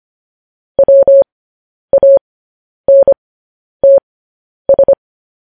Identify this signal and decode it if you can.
Morse code, heard as a keyed tone, about 25 wpm, spaced as for 10 wpm, 550 Hz. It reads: WANTS
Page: news-headlines-in-morse-code-at-10-wpm